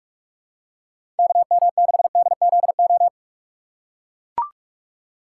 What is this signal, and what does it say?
Morse code, heard as a keyed tone, about 43 wpm, 700 Hz. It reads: KM6DZO
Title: USA Call Signs - 6 Characters - 3x Character Spacing - 45wpm